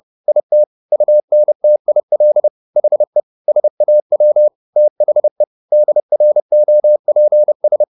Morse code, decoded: IT UNTIL HE SAW THE DROPS

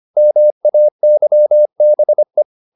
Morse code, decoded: MAYBE

